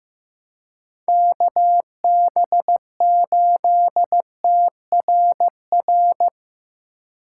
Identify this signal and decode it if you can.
Morse code, heard as a keyed tone, about 18 wpm, 700 Hz. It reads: KB8TRR
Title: Speed Racing - USA 2x3 Call Signs 15wpm